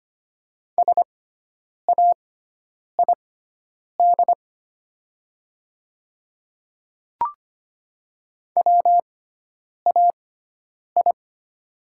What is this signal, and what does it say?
Morse code, heard as a keyed tone, about 25 wpm, spaced as for 9 wpm, 700 Hz. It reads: SAID WAI
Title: 6x Character Spacing - Top 500 Words - 25wpm